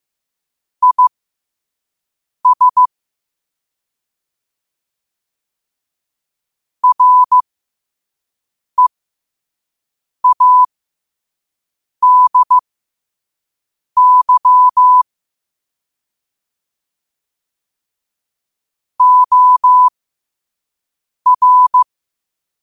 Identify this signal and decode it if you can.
Morse code, heard as a keyed tone, about 15 wpm, spaced as for 5 wpm, 1000 Hz. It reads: IS READY OR
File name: qotdfiles / 5wpm